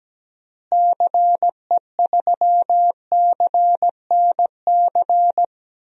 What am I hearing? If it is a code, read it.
Morse code, heard as a keyed tone, about 17 wpm, 700 Hz. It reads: CE3CNC
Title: DXCC-100 Call Signs - 6 Characters - Repeated 3x - 17wpm